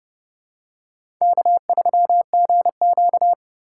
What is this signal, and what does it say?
Morse code, heard as a keyed tone, about 30 wpm, 700 Hz. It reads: K3GQ
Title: USA Call Signs - 4 Characters - 2x Character Spacing - 30wpm